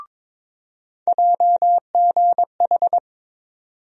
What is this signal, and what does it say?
Morse code, heard as a keyed tone, about 22 wpm, 700 Hz. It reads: JGH
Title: USA Call Sign Postfixes - 3 Letters 22wpm